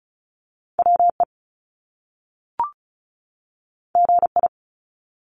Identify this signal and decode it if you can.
Morse code, heard as a keyed tone, about 36 wpm, 700 Hz. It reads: WE GI
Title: Top Two Letter Combinations in Words 35wpm